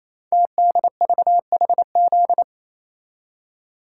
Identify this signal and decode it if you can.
Morse code, heard as a keyed tone, about 27 wpm, 690 Hz. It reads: TDVHZ
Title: Code Groups - 5 Letters - 28wpm